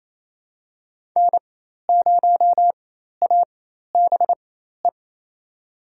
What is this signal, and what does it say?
Morse code, heard as a keyed tone, about 27 wpm, spaced as for 13 wpm, 700 Hz. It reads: N0ABE
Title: USA Call Signs - 5 Characters - 4x Character Spacing - 28wpm